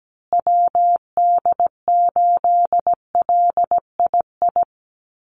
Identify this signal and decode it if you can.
Morse code, heard as a keyed tone, about 17 wpm, 700 Hz. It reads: WD8LII
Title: USA Call Signs Repeated 2x 17wpm